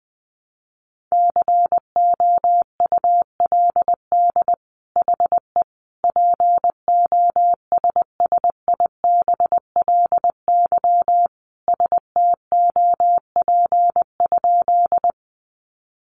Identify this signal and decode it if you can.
Morse code, heard as a keyed tone, about 20 wpm, 700 Hz. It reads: COULD HE POSSIBLY STOP?
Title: Sentences from Top 1300 Words 20wpm